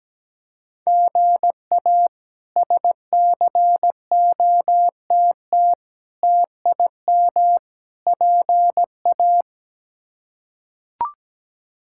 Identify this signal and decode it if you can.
Morse code, heard as a keyed tone, about 17 wpm, 700 Hz. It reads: GA SCOTT TIM PA
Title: K1USN SST QSO Practice - 17wpm